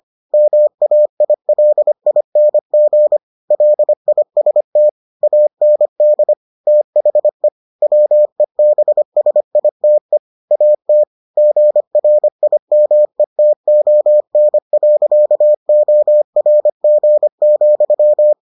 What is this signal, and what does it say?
Morse code, heard as a keyed tone, about 25 wpm, 600 Hz. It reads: MAILING LIST AND THE WEBSITE AT GRIMETON.ORG,